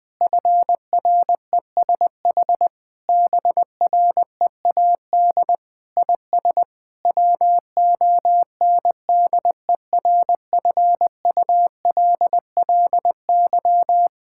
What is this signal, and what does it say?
Morse code, heard as a keyed tone, about 19 wpm, 700 Hz. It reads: FRESH BREAD IS WONDERFULLY
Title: Difficult Sentences with Hints - 20wpm